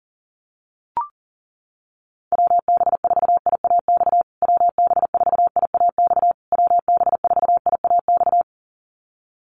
Morse code, heard as a keyed tone, about 37 wpm, 700 Hz. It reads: WB4IAX WB4IAX WB4IAX
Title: USA Call Signs Repeated 3x 40wpm